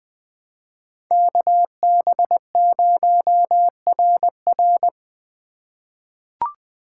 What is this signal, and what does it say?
Morse code, heard as a keyed tone, about 20 wpm, 700 Hz. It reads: KB0RR
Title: USA Call Signs - 5 Characters - 2x Character Spacing - 20wpm